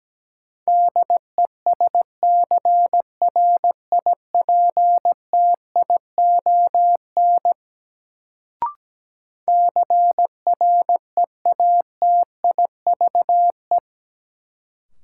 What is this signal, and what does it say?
Morse code, heard as a keyed tone, about 17 wpm, 700 Hz. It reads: DESCRIPTION CREATIVE
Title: Sentences from Top 1300 Words - New Word Drill Alt - 17wpm